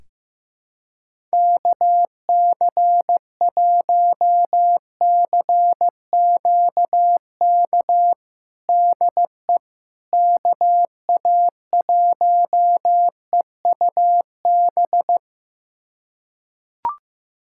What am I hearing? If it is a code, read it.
Morse code, heard as a keyed tone, about 16 wpm, 700 Hz. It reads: KC1CQK DE KA1EUB
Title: USA Callsign DE Callsign 15wpm